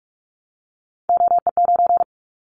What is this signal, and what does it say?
Morse code, heard as a keyed tone, about 44 wpm, 700 Hz. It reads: OE9
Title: Callsign Prefixes for Top 25 DXCC - Repeated 3x - 45wpm